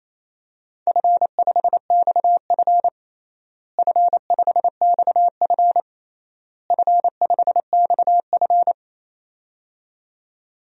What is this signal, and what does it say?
Morse code, heard as a keyed tone, about 27 wpm, 700 Hz. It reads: F5XF F5XF F5XF
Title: DXCC-100 Call Signs - 4 Characters - Repeated 3x - 28wpm